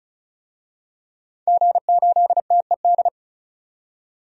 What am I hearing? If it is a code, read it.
Morse code, heard as a keyed tone, about 36 wpm, 700 Hz. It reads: G8TED
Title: DXCC-100 Call Signs - 5 Characters - 35wpm